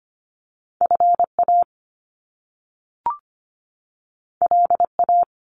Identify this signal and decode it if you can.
Morse code, heard as a keyed tone, about 25 wpm, 700 Hz. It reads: FA LA
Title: Top Two Letter Combinations in Words 25wpm